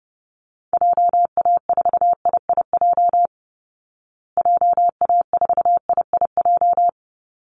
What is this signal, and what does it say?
Morse code, heard as a keyed tone, about 30 wpm, 700 Hz. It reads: JA4IIJ JA4IIJ
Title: Call Signs Repeated 3x - 30wpm